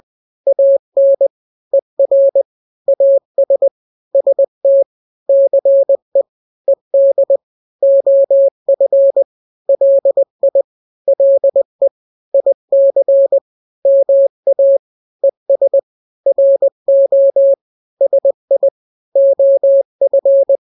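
Morse code, heard as a keyed tone, about 20 wpm, 550 Hz. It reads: AN ER AS ST CE ED OF LI LE IC MA ES RO SI OF
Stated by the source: Morse drill of two-letter n-grams at 20/18 WPM (Farnsworth), from KK7RR.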